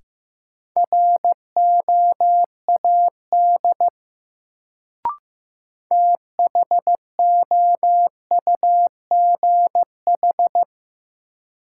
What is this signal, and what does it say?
Morse code, heard as a keyed tone, about 15 wpm, 700 Hz. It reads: ROAD THOUGH
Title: Sentences from Top 400 Words - New Word Drill - 15wpm